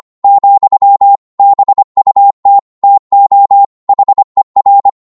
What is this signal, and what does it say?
Morse code, heard as a keyed tone, about 25 wpm, 800 Hz. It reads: , BUT TO HER